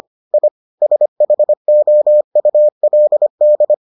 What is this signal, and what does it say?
Morse code, heard as a keyed tone, about 25 wpm, 600 Hz. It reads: I SHOULD